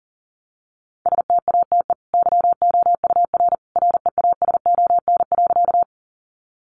Morse code, heard as a keyed tone, about 40 wpm, 700 Hz. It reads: STATE YOUR REASON.
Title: Sentences from Top 700 Words 40wpm